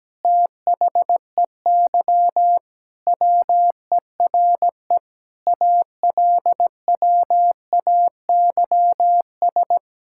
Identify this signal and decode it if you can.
Morse code, heard as a keyed tone, about 17 wpm, 700 Hz. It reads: THEY WERE ALWAYS
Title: Sentences from Top 1000 Words - Review Part 5 - 17wpm